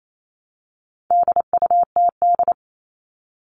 Morse code, heard as a keyed tone, about 28 wpm, 700 Hz. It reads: DUTD
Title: All Cut Numbers - 4-Digit 28wpm